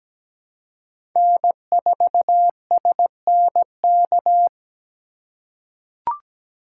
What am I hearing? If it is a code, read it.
Morse code, heard as a keyed tone, about 17 wpm, 700 Hz. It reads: N4SNK
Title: USA Call Signs - 3x Character Spacing - 17wpm